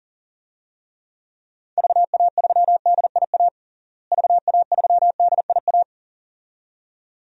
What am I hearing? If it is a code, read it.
Morse code, heard as a keyed tone, about 40 wpm, 700 Hz. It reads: VA3DIA VA3DIA
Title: DXCC-100 Call Signs - 6 Characters - Repeated 2x - 40wpm